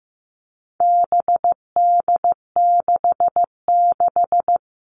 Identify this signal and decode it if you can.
Morse code, heard as a keyed tone, about 15 wpm, 700 Hz. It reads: BD66
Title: All Cut Numbers - 4-Digit 15wpm